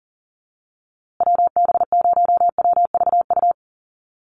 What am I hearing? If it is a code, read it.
Morse code, heard as a keyed tone, about 40 wpm, 700 Hz. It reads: WB0WVU
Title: USA Call Signs Repeated 2x 40wpm